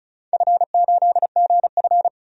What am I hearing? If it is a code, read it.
Morse code, heard as a keyed tone, about 36 wpm, 690 Hz. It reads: F8GF
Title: DXCC-100 Call Signs - 4 Characters - 35wpm